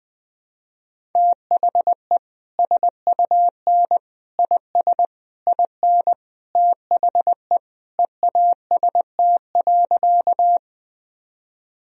Morse code, heard as a keyed tone, about 21 wpm, 700 Hz. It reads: THE SUN IS IN THE EAST.
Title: Sentences from Top 1000 Words - Review Part 3 - 20wpm